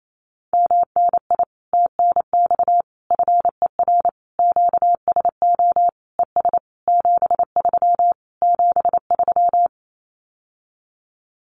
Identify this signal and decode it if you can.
Morse code, heard as a keyed tone, about 28 wpm, 700 Hz. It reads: MNI TNX FER QSO ES 73 73
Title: Common QSO Phrases Encore 28wpm